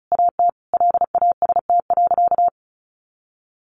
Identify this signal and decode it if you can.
Morse code, heard as a keyed tone, about 34 wpm, 700 Hz. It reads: AT LAST.
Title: Sentences from Top 1100 Words 35wpm